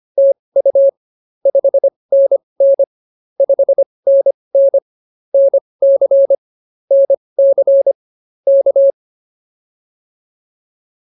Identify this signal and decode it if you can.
Morse code, heard as a keyed tone, about 25 wpm, spaced as for 20 wpm, 550 Hz. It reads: TU 5NN 5NN NC NC K